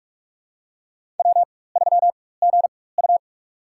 Morse code, heard as a keyed tone, about 44 wpm, spaced as for 21 wpm, 700 Hz. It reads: W3GU
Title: USA Call Signs - 4x Character Spacing - 45wpm